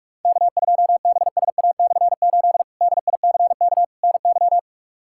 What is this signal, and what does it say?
Morse code, heard as a keyed tone, about 46 wpm, 700 Hz. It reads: K2BSA/8 DICK NY
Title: K1USN SST QSO Practice - 45wpm